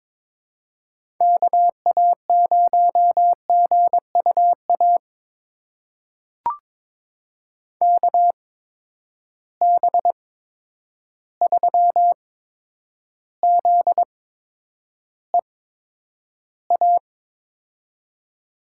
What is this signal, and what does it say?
Morse code, heard as a keyed tone, about 22 wpm, spaced as for 9 wpm, 700 Hz. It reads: KA0GUA KB3ZEA
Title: USA Call Signs - 6 Characters - 8x Character Spacing - 22wpm